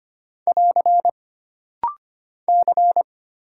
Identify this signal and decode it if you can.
Morse code, heard as a keyed tone, about 25 wpm, 700 Hz. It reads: <AR> C
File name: Single Character (Letter:Number:Punctuation:Prosign) - Rapid-Fire - 25wpm